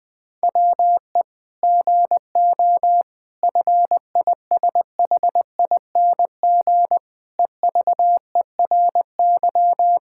Morse code, heard as a keyed tone, about 20 wpm, 700 Hz. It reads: WE GO FISHING EVERY
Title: Sentences from Top 1000 Words - Review Part 5 - 20wpm